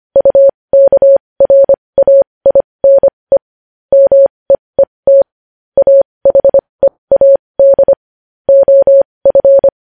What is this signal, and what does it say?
Morse code, heard as a keyed tone, about 25 wpm, spaced as for 20 wpm, 550 Hz. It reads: UKRAINE MEET AHEAD OF
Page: news-headlines-in-morse-code-at-20-wpm